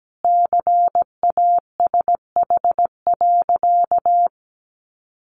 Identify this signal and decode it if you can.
Morse code, heard as a keyed tone, about 17 wpm, 700 Hz. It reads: CASH.
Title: Sentences from Top 1300 Words 17wpm